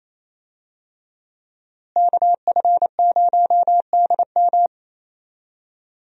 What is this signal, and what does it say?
Morse code, heard as a keyed tone, about 28 wpm, 700 Hz. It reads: KF0DM
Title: USA Call Signs - 5 Characters - Repeated 2x - 28wpm